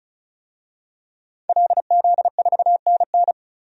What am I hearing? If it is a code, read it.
Morse code, heard as a keyed tone, about 34 wpm, 700 Hz. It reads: LZ4NN
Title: DXCC-100 Call Signs - 5 Characters - 35wpm